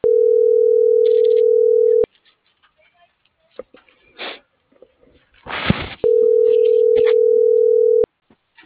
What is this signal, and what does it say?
Ringback tone (North America), 440 + 480 Hz.